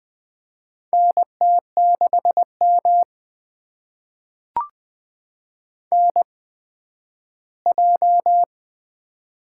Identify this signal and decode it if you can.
Morse code, heard as a keyed tone, about 20 wpm, spaced as for 12 wpm, 700 Hz. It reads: NT6M NJ
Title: USA Call Signs - 4 Characters - 8x Character Spacing - 20wpm